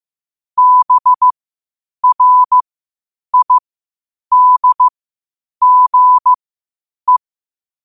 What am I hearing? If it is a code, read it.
Morse code, heard as a keyed tone, about 15 wpm, spaced as for 8 wpm, 1000 Hz. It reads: BRIDGE
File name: qotdfiles / 8wpm